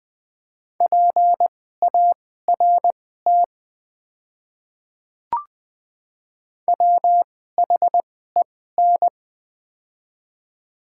Morse code, heard as a keyed tone, about 20 wpm, spaced as for 14 wpm, 690 Hz. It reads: PART WHEN
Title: 2x Character Spacing - Top 500 Words - 20wpm